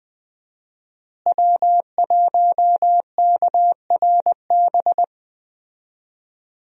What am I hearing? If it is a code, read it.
Morse code, heard as a keyed tone, about 20 wpm, 700 Hz. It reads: W1KRB
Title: USA Call Signs - 5 Characters - Repeated 2x - 20wpm